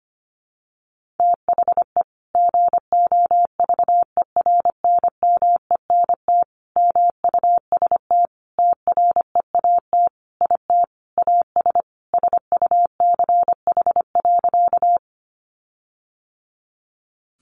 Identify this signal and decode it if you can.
Morse code, heard as a keyed tone, about 25 wpm, 700 Hz. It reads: THE GOVERNMENT MUST TREAT IT AS SUCH.
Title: Sentences from Top 1300 Words 25wpm